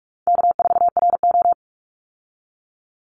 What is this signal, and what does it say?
Morse code, heard as a keyed tone, about 44 wpm, 700 Hz. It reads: K4RO
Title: Call Signs Repeated 3x - 45wpm